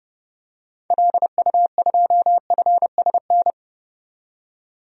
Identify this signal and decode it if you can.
Morse code, heard as a keyed tone, about 30 wpm, 700 Hz. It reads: LU2FSN
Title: DXCC-100 Call Signs - 6 Characters - 30wpm